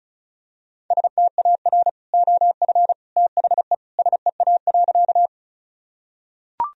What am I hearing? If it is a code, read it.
Morse code, heard as a keyed tone, about 34 wpm, 700 Hz. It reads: STAR OF THE SEA.
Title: Sentences from Top 1000 Words - Review Part 2 - 35wpm